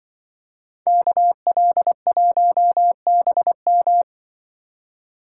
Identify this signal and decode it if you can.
Morse code, heard as a keyed tone, about 23 wpm, 700 Hz. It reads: KL1BM
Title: Speed Racing - USA 2x2 Call Signs 20wpm